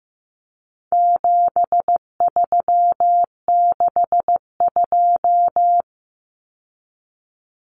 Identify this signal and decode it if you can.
Morse code, heard as a keyed tone, about 15 wpm, 700 Hz. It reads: 7362
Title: Numbers - 4-Digit 15wpm